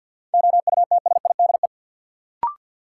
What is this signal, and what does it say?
Morse code, heard as a keyed tone, about 51 wpm, 700 Hz. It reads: OUTSIDE
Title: Sentences from Top 900 Words - New Word Drill - 50wpm